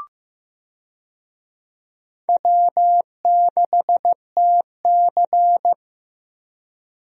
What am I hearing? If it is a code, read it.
Morse code, heard as a keyed tone, about 15 wpm, 700 Hz. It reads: W6TC
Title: USA Call Signs - 4 Characters - Repeated 2x - 15wpm